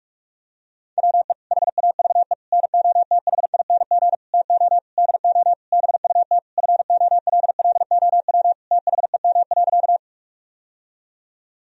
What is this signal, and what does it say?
Morse code, heard as a keyed tone, about 46 wpm, 700 Hz. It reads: WE HAVE NOTHING TO DO BUT FOLLOW THEM.
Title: Sentences from Top 1000 Words - Review Part 1 - 45wpm